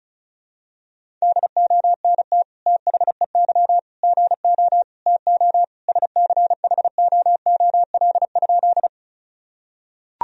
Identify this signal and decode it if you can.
Morse code, heard as a keyed tone, about 34 wpm, 700 Hz. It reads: DONT THEY GO TO SCHOOL?
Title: Sentences from Top 1000 Words - Review Part 1 - 35wpm